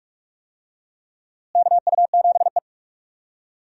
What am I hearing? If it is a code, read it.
Morse code, heard as a keyed tone, about 41 wpm, 700 Hz. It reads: KU7E
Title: USA Call Signs - 4 Characters - Repeated 2x - 45wpm